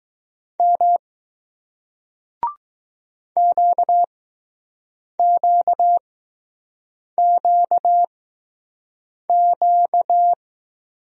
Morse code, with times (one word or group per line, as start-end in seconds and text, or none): 0.6-1.0 M
3.4-10.3 QQQQ